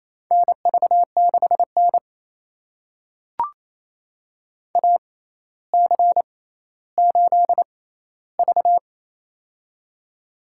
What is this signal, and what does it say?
Morse code, heard as a keyed tone, about 28 wpm, spaced as for 14 wpm, 700 Hz. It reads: NV6N AC8V
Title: USA Call Signs - 4 Characters - 6x Character Spacing - 28wpm